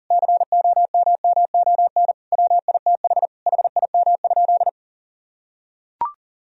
Morse code, heard as a keyed tone, about 40 wpm, 700 Hz. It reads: COMMON WITH HIM?
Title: Sentences from Top 1000 Words - Review Part 3 - 40wpm